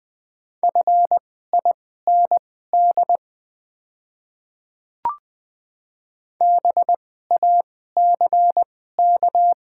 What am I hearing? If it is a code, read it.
Morse code, heard as a keyed tone, about 20 wpm, spaced as for 14 wpm, 690 Hz. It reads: FIND BACK
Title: 2x Character Spacing - Top 500 Words - 20wpm